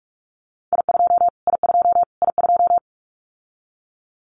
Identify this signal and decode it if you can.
Morse code, heard as a keyed tone, about 44 wpm, 700 Hz. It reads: I2 I2 I2
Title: Callsign Prefixes for Top 25 DXCC - Repeated 3x - 45wpm